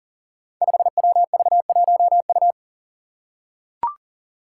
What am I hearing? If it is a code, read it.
Morse code, heard as a keyed tone, about 40 wpm, 700 Hz. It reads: 5WV1U
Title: Five Challenging Letter-Number Combinations 40wpm